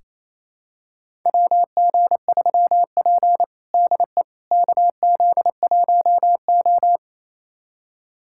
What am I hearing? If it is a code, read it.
Morse code, heard as a keyed tone, about 28 wpm, 690 Hz. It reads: WG3P DE KZ1O
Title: USA Callsign DE Callsign - 4 Characters - 28wpm